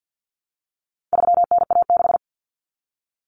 Morse code, heard as a keyed tone, about 50 wpm, 700 Hz. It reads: 3NA6
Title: Common Cut Numbers - 4-Digit 50wpm